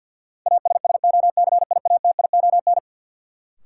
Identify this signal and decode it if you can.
Morse code, heard as a keyed tone, about 52 wpm, 700 Hz. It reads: ASSOCIATION
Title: Sentences from Top 1100 Words - New Word Drill Alt - 50wpm